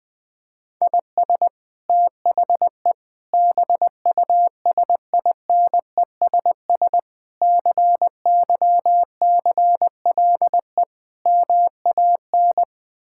Morse code, heard as a keyed tone, about 20 wpm, 700 Hz. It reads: IS THE BUSINESS CYCLE MAN
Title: Sentences from Top 1000 Words - Review Part 6 - 20wpm